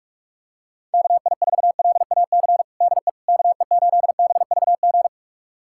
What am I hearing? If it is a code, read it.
Morse code, heard as a keyed tone, about 42 wpm, 690 Hz. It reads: KI4LAC DE KE8BVG
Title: USA Callsign DE Callsign - 6 Characters - 45wpm